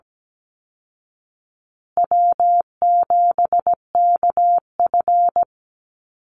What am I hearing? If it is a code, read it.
Morse code, heard as a keyed tone, about 17 wpm, 700 Hz. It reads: W7KF
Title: Easy Call Signs Repeated 2x - 17wpm